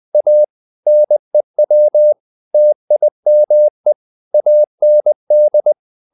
Morse code, heard as a keyed tone, about 20 wpm, 600 Hz. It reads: A NEW TIME AND
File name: the old man and the sea / the old man and the sea part 1 20wpm